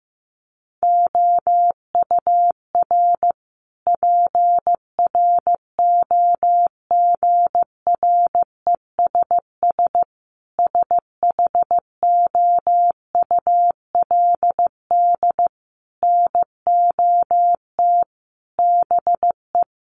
Morse code, heard as a keyed tone, about 15 wpm, 700 Hz. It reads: OUR PROGRESS SHOULD NOT BE
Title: Sentences from Top 1400 Words 15wpm